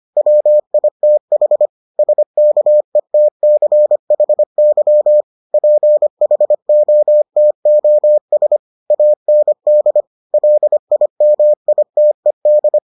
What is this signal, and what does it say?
Morse code, heard as a keyed tone, about 25 wpm, 600 Hz. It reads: WITH SKETCHY PHOTOS AND LIMITED